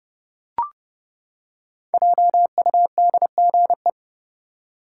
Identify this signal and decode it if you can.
Morse code, heard as a keyed tone, about 31 wpm, 700 Hz. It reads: JUDGE
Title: Five Letter Words - 30wpm